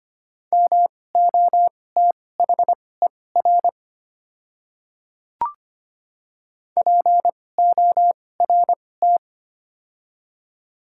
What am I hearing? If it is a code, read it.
Morse code, heard as a keyed tone, about 25 wpm, spaced as for 18 wpm, 700 Hz. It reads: MOTHER PORT
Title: 2x Character Spacing - Top 300 Words - 25wpm